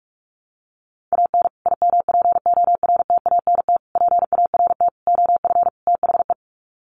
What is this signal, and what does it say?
Morse code, heard as a keyed tone, about 44 wpm, 700 Hz. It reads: AN IMPORTANT PART OF THE
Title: Sets of 5 Words Encore 45wpm